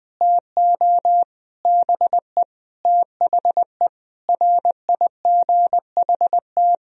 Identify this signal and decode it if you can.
Morse code, heard as a keyed tone, about 20 wpm, 700 Hz. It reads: TO BE THE RIGHT
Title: Sentences from Top 1000 Words - Review Part 2 - 20wpm